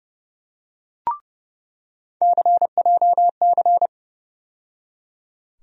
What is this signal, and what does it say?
Morse code, heard as a keyed tone, about 30 wpm, 700 Hz. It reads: CJC
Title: USA Call Sign Postfixes - 3 Letters 30wpm